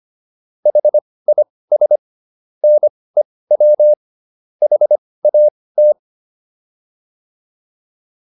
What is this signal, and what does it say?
Morse code, heard as a keyed tone, about 25 wpm, spaced as for 18 wpm, 600 Hz. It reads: HIS NEW HAT